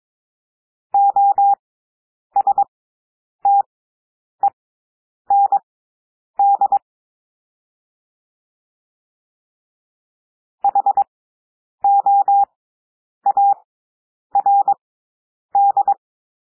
Morse code, heard as a keyed tone, about 22 wpm, spaced as for 9 wpm, 800 Hz. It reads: OSTEND HOARD
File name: cwmp3 / W2 SOUND WORDS 22 wpm